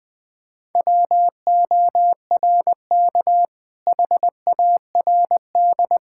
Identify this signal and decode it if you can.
Morse code, heard as a keyed tone, about 20 wpm, 700 Hz. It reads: WORK HARD
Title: Sentences with 2-Word Phrases - Spoken Beginning - 20wpm